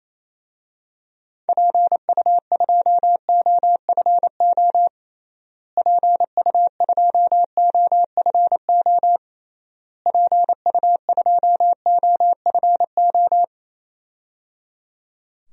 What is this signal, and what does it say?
Morse code, heard as a keyed tone, about 28 wpm, 700 Hz. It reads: PU2OFO PU2OFO PU2OFO
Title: DXCC-100 Call Signs - 6 Characters - Repeated 3x - 28wpm